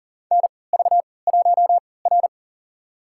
Morse code, heard as a keyed tone, about 40 wpm, spaced as for 22 wpm, 700 Hz. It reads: NV1R
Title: USA Call Signs - 4 Characters - 3x Character Spacing - 40wpm